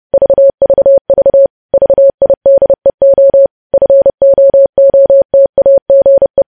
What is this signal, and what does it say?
Morse code, heard as a keyed tone, about 30 wpm, 550 Hz. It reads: VVV VIDEO FOOTAGE
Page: news headlines in morse code at 30 wpm